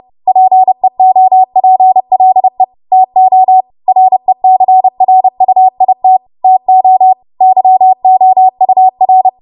Morse code, heard as a keyed tone, about 30 wpm, 750 Hz. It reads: PEOPLE TO RECRUIT TO YOUR
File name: Morse / 30WPM